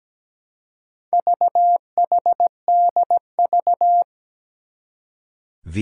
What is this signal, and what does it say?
Morse code, heard as a keyed tone, about 17 wpm, 700 Hz. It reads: VHDV
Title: Four Challenging Letter-Number Combinations 17wpm